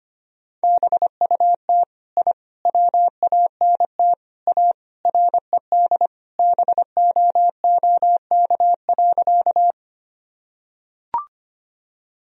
Morse code, heard as a keyed tone, about 25 wpm, 700 Hz. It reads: BUT I WANT A RED BOOK.
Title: Sentences from Top 1000 Words - Review Part 2 - 25wpm